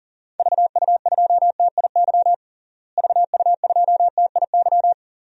Morse code, heard as a keyed tone, about 40 wpm, 700 Hz. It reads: VU2TIY VU2TIY
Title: DXCC-100 Call Signs - 6 Characters - Repeated 3x - 40wpm